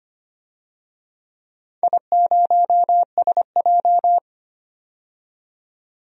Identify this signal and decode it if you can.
Morse code, heard as a keyed tone, about 25 wpm, 700 Hz. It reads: I0SJ
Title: DXCC-100 Call Signs - 4 Characters - 25wpm